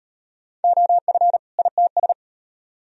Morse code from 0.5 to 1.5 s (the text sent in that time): OF